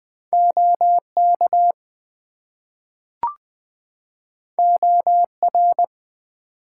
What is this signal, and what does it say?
Morse code, heard as a keyed tone, about 20 wpm, 700 Hz. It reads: OK OR
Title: Two Letter Words - 20wpm